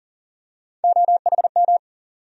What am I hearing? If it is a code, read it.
Morse code, heard as a keyed tone, about 40 wpm, 700 Hz. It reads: OHM
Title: USA Call Sign Postfixes - 3 Letters 40wpm